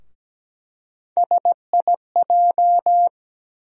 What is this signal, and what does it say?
Morse code, heard as a keyed tone, about 17 wpm, 700 Hz. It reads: SIJ
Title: USA Call Sign Postfixes - 3 Letters 17wpm